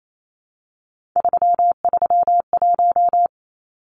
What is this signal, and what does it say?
Morse code, heard as a keyed tone, about 28 wpm, 700 Hz. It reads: 331
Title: Numbers - 3-Digit 28wpm